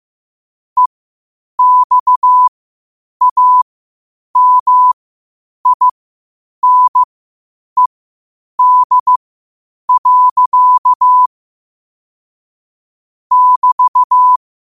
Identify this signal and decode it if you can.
Morse code, heard as a keyed tone, about 15 wpm, spaced as for 8 wpm, 1000 Hz. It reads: EXAMINED. <BT>
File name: qotdfiles / 8wpm